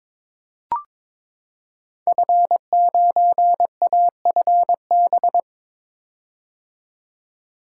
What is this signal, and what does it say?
Morse code, heard as a keyed tone, about 21 wpm, 700 Hz. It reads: F9AFB
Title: Code Groups - 5 Letter:Numbers - 22wpm